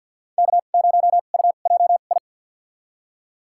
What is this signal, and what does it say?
Morse code, heard as a keyed tone, about 52 wpm, spaced as for 37 wpm, 690 Hz. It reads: K0UJI
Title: USA Call Signs - 5 Characters - 2x Character Spacing - 50wpm